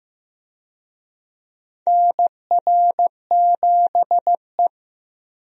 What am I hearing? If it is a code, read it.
Morse code, heard as a keyed tone, about 15 wpm, 700 Hz. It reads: NR7E
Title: USA Call Signs - 4 Characters - 15wpm